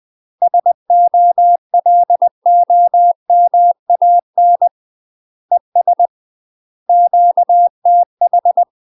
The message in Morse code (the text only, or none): SOLOMAN ES QTH